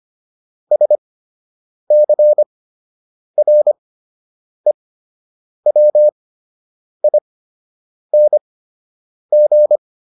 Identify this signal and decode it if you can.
Morse code, heard as a keyed tone, about 25 wpm, spaced as for 8 wpm, 600 Hz. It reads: SCREWING